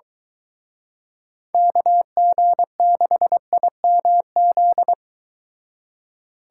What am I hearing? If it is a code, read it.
Morse code, heard as a keyed tone, about 23 wpm, 700 Hz. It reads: KG6IMZ